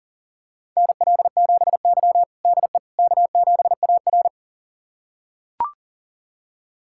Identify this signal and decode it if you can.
Morse code, heard as a keyed tone, about 40 wpm, 700 Hz. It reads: NL7Y DE K7AR
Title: USA Callsign DE Callsign - 4 Characters - 40wpm